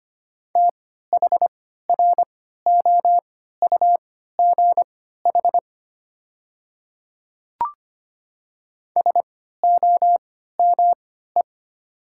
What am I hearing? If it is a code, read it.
Morse code, heard as a keyed tone, about 25 wpm, spaced as for 14 wpm, 700 Hz. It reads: THROUGH SOME
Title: 3x Character Spacing - Top 200 Words - 25wpm